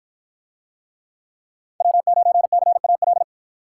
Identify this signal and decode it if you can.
Morse code, heard as a keyed tone, about 53 wpm, 700 Hz. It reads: W9CAL